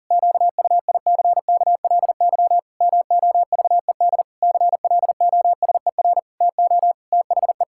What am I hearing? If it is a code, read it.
Morse code, heard as a keyed tone, about 40 wpm, 700 Hz. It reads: QUICKLY MOVED CLOSER TO THE
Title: Pangram Challenge v2 - 40wpm